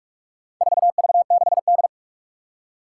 Morse code, heard as a keyed tone, about 44 wpm, 700 Hz. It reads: 4V6D
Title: Four Challenging Letter-Number Combinations 45wpm